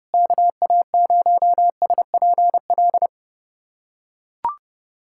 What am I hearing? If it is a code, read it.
Morse code, heard as a keyed tone, about 32 wpm, 700 Hz. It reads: KA0SPL
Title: USA Callsign DE Callsign - 6 Characters - 30wpm